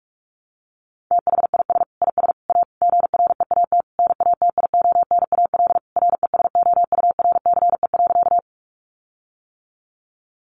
Morse code, heard as a keyed tone, about 44 wpm, 700 Hz. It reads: THIS IS A GREAT NATIONAL RESOURCE.
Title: Sentences from Top 1300 Words 45wpm